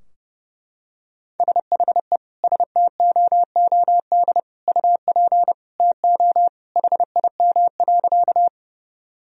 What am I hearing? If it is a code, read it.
Morse code, heard as a keyed tone, about 30 wpm, 700 Hz. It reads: SHE STOOD UP TO HIM.
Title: Sentences from Top 1000 Words - Review Part 2 - 30wpm